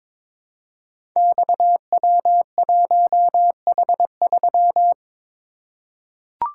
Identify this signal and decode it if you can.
Morse code, heard as a keyed tone, about 22 wpm, 700 Hz. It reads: XW1H3
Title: Code Groups - 5 Letter:Numbers - 22wpm